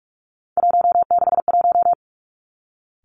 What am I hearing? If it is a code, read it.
Morse code, heard as a keyed tone, about 46 wpm, 700 Hz. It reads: 161
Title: Three Challenging Letter-Number Combinations 45wpm